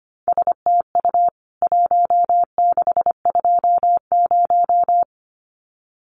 Morse code, heard as a keyed tone, about 25 wpm, 700 Hz. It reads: STU 1620
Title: CWT Exchange 25wpm